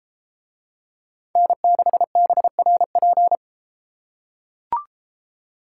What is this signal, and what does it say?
Morse code, heard as a keyed tone, about 32 wpm, 700 Hz. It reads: N6BRP